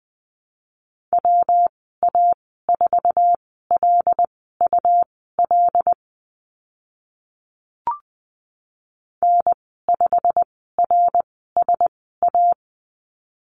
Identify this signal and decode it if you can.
Morse code, heard as a keyed tone, about 20 wpm, spaced as for 14 wpm, 700 Hz. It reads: WA4LUL N5RSA
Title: 128 Lesson 21g - Number 6 - Call Signs 20wpm